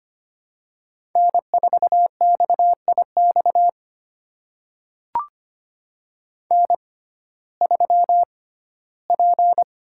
Morse code, heard as a keyed tone, about 25 wpm, spaced as for 15 wpm, 700 Hz. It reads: N4XIX N3P
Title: USA Call Signs - 5 Characters - 6x Character Spacing - 25wpm